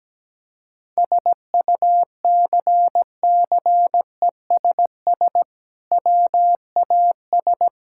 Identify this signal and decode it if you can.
Morse code, heard as a keyed tone, about 17 wpm, 700 Hz. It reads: SUCCESS WAS
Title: Sentences from Top 1000 Words - Review Part 6 - 17wpm